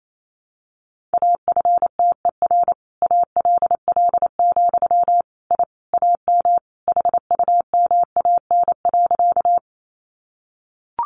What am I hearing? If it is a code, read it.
Morse code, heard as a keyed tone, about 28 wpm, 700 Hz. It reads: AFTER ALL, I AM HUMAN.
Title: Sentences from Top 600 Words 28wpm